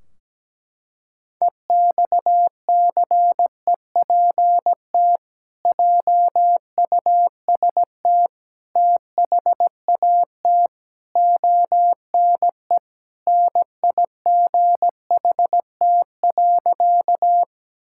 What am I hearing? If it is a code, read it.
Morse code, heard as a keyed tone, about 17 wpm, 700 Hz. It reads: EXCEPT JUST THAT ONE NIGHT.
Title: Sentences from Top 1600 Words 17wpm